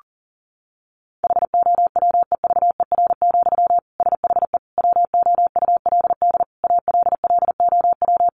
Morse code, heard as a keyed tone, about 40 wpm, 700 Hz. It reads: HOWEVER, SHE WOULD ALLOW
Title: Sentences from Top 700 Words 40wpm